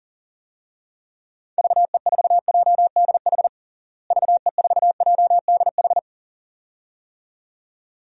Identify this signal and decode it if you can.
Morse code, heard as a keyed tone, about 40 wpm, 700 Hz. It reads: VE4JDH VE4JDH
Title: DXCC-100 Call Signs - 6 Characters - Repeated 2x - 40wpm